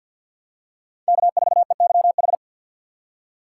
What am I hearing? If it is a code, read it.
Morse code, heard as a keyed tone, about 51 wpm, 700 Hz. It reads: K4EYH
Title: USA Call Signs - 5 Characters - 4x Character Spacing - 50wpm